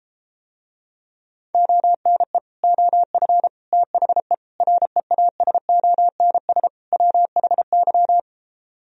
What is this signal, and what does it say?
Morse code, heard as a keyed tone, about 33 wpm, 700 Hz. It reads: ONE OF THE REASONS WHY